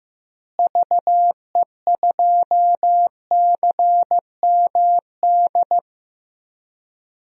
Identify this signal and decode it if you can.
Morse code, heard as a keyed tone, about 15 wpm, 700 Hz. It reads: VE2CMD
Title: DXCC-100 Call Signs - 6 Characters - Repeated 3x - 15wpm